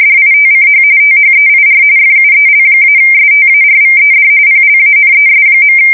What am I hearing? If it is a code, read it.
RTTY, about 45.45 baud. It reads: DITCH THEIR FUEL-STARVED PLANES.
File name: fsk170 / battleshipnj